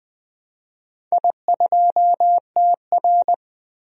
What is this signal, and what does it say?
Morse code, heard as a keyed tone, about 20 wpm, 700 Hz. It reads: I2TR
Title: DXCC-100 Call Signs - 4 Characters - 20wpm